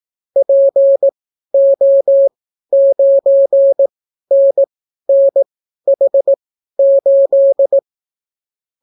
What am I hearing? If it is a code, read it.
Morse code, heard as a keyed tone, about 18 wpm, spaced as for 12 wpm, 550 Hz. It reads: PO9NNH8